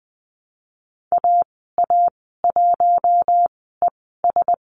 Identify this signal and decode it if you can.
Morse code, heard as a keyed tone, about 20 wpm, spaced as for 14 wpm, 700 Hz. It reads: AA1ES
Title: 055 Lesson 10g - Letter r - Call Signs 20wpm